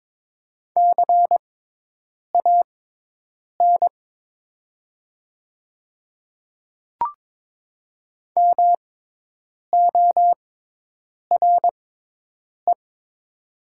Morse code, heard as a keyed tone, about 22 wpm, spaced as for 8 wpm, 700 Hz. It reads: CAN MORE
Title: 6x Character Spacing - Top 100 Words - 22wpm